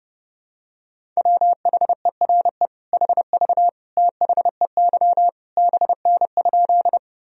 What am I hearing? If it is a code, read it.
Morse code, heard as a keyed tone, about 30 wpm, 700 Hz. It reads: WHERE HV THEY BN?
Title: Sentences from Top 200 Words and Abbreviations 30wpm